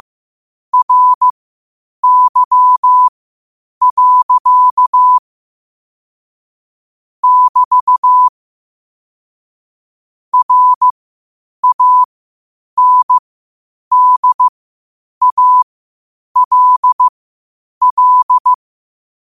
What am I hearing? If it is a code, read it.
Morse code, heard as a keyed tone, about 15 wpm, spaced as for 8 wpm, 1000 Hz. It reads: RY. <BT> RANDALL
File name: qotdfiles / 8wpm